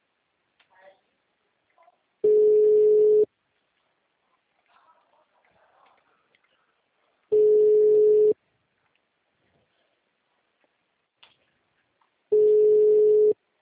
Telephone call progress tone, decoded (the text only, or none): ringback tone